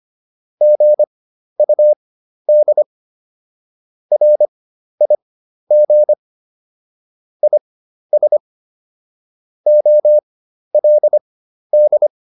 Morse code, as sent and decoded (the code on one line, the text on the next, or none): --. ..- -.. / .-. .. --. / .. ... / --- .-.. -..
GUD RIG IS OLD